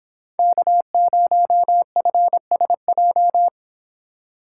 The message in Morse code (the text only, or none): K0FSJ